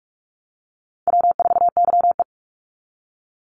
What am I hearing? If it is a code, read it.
Morse code, heard as a keyed tone, about 44 wpm, 700 Hz. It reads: W4YE
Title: Easy Call Signs - 45wpm